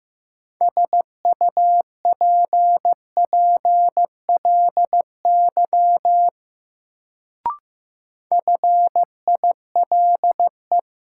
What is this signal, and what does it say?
Morse code, heard as a keyed tone, about 15 wpm, 700 Hz. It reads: SUPPLY FILE
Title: Sentences from Top 1000 Words - Review Part 4 - New Word Drill - 15wpm